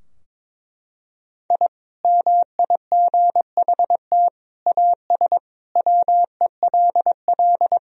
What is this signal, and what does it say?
Morse code, heard as a keyed tone, about 22 wpm, 700 Hz. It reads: I MIGHT AS WELL
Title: Sentences from Top 1000 Words - Review Part 2 - 22wpm